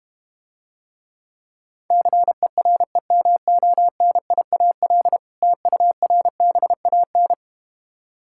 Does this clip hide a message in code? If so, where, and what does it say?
Yes: Morse code from 1.9 to 7.3 s, heard as a keyed tone, about 32 wpm, 700 Hz. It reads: CEREMONIAL TURBAN